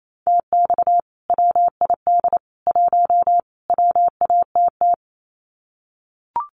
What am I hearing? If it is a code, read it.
Morse code, heard as a keyed tone, about 28 wpm, 700 Hz. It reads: TX WID 1 WATT
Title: Common QSO Phrases 28wpm